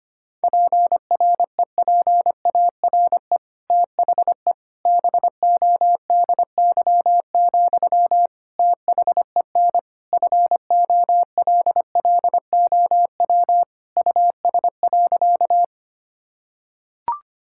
Morse code, heard as a keyed tone, about 25 wpm, 700 Hz. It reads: PREPARE THE BODY, THEN FOLLOW US.
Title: Sentences from Top 1000 Words - Review Part 6 - 25wpm